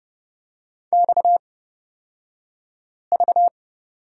Morse code, heard as a keyed tone, about 30 wpm, spaced as for 5 wpm, 700 Hz. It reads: XV
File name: Lesson 37d - Letter x - Two Characters 30wpm